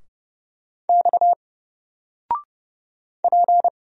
Morse code, heard as a keyed tone, about 30 wpm, 700 Hz. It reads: X P
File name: Single-Letter-Number-30wpm